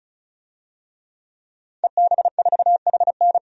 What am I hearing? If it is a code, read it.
Morse code, heard as a keyed tone, about 36 wpm, 700 Hz. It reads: EB4HN
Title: DXCC-100 Call Signs - 5 Characters - 35wpm